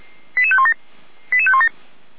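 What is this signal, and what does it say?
Five-tone selective call (ZVEI), heard as quick runs of five tones, tones of about 70 ms. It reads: 88517 88517
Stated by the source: ZVEI five-tone selective call received on SDR.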